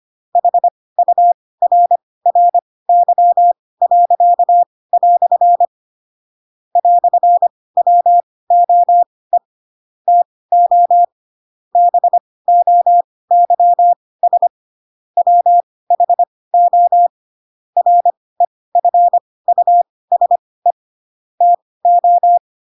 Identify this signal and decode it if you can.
Morse code, heard as a keyed tone, about 25 wpm, spaced as for 18 wpm, 700 Hz. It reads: HURRY." "WOE TO BOYS WHO REFUSE TO
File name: Pinocchio / Pinocchio ch4 12 25 ik1hge